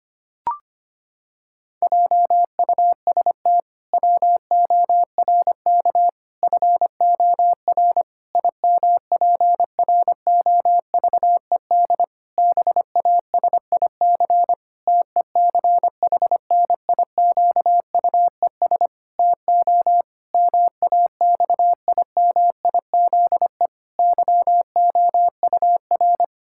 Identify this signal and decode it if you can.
Morse code, heard as a keyed tone, about 25 wpm, 700 Hz. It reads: JUST WORK FOR IMPROVED BASIC TECHNIQUES TO MAXIMIZE YOUR
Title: Pangram Challenge v2 - 25wpm